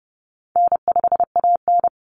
Morse code, heard as a keyed tone, about 30 wpm, 700 Hz. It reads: N5AN
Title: Call Signs Repeated 3x - 30wpm